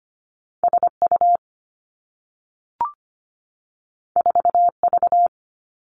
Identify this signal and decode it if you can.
Morse code, heard as a keyed tone, about 25 wpm, 700 Hz. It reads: S U 4 V
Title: Two Challenging Letter-Number Combinations 25wpm